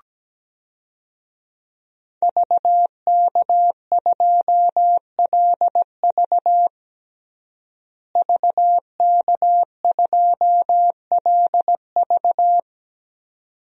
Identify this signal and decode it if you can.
Morse code, heard as a keyed tone, about 17 wpm, 700 Hz. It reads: VK2LV VK2LV
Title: DXCC-100 Call Signs - 5 Characters - Repeated 2x - 17wpm